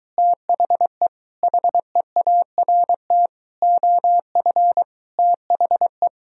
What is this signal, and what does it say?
Morse code, heard as a keyed tone, about 23 wpm, 700 Hz. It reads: THE HEART OF THE